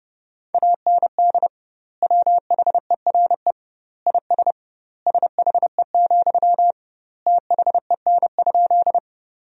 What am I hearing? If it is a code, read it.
Morse code, heard as a keyed tone, about 30 wpm, 700 Hz. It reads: AND WHERE IS SHE, THEN?
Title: 2x Word-Spacing - Sentences from Top 200 Words - 30wpm